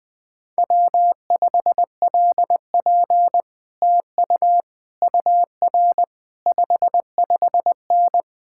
Morse code, heard as a keyed tone, about 20 wpm, 700 Hz. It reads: W5LP TU UR 55N
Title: Speed-Racing Common QSO Phrases - 20wpm